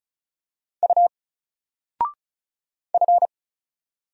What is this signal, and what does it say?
Morse code, heard as a keyed tone, about 36 wpm, 700 Hz. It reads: U F
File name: Single-Letter-Number-35wpm